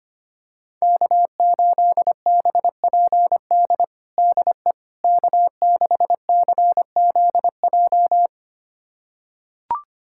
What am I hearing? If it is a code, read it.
Morse code, heard as a keyed tone, about 25 wpm, 700 Hz. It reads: K8BPD DE K6CZJ
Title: USA Callsign DE Callsign - 5 Characters - 25wpm